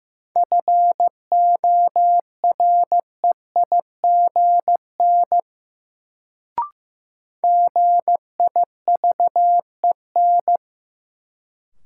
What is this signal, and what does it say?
Morse code, heard as a keyed tone, about 15 wpm, 700 Hz. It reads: FOREIGN GIVEN
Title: Sentences from Top 1400 Words - New Word Drill Alt - 15wpm